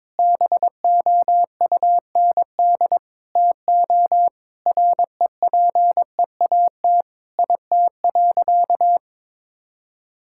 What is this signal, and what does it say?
Morse code, heard as a keyed tone, about 22 wpm, 700 Hz. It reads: BOUND TO REPEAT IT.
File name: practice / 22wpm / Sentences with 8-Word Phrases - 22wpm